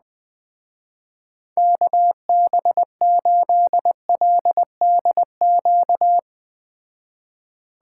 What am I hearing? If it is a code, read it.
Morse code, heard as a keyed tone, about 20 wpm, 700 Hz. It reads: KB8LDQ
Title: Speed Racing - USA Call Signs 17wpm